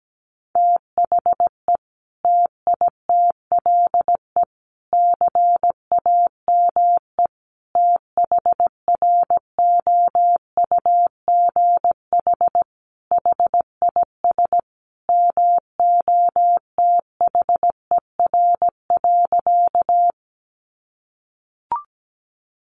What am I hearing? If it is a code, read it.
Morse code, heard as a keyed tone, about 17 wpm, 700 Hz. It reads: THE TITLE CAME THROUGH HIS MOTHER.
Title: Sentences from Top 1200 Words 17wpm